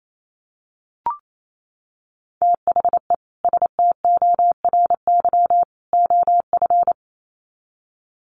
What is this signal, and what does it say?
Morse code, heard as a keyed tone, about 28 wpm, 700 Hz. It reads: THE STORY OF
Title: Sets of 3 Words 28wpm